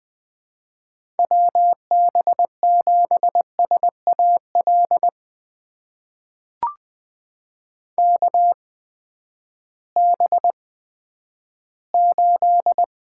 Morse code, heard as a keyed tone, about 21 wpm, spaced as for 11 wpm, 700 Hz. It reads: WB7SAL KB8
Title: USA Call Signs - 6 Characters - 8x Character Spacing - 20wpm